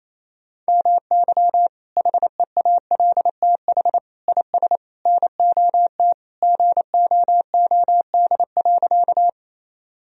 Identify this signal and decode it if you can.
Morse code, heard as a keyed tone, about 28 wpm, 700 Hz. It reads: MY HEALTH IS NOT GOOD.
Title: Sentences from Top 1000 Words - Review Part 3 - 28wpm